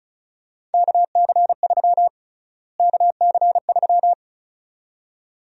Morse code, heard as a keyed tone, about 36 wpm, 700 Hz. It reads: KC3 KC3
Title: USA Call Sign Prefixes - Repeated 2x - 35wpm